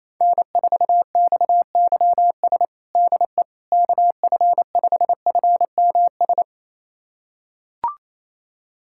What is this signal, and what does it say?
Morse code, heard as a keyed tone, about 28 wpm, 700 Hz. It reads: N4XYS DE KF5FMS
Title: USA Callsign DE Callsign 28wpm